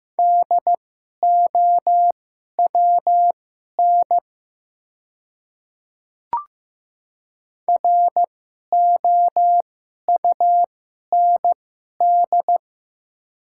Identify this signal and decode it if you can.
Morse code, heard as a keyed tone, about 15 wpm, spaced as for 11 wpm, 700 Hz. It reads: DOWN ROUND
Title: 2x Character Spacing - Top 200 Words - 15wpm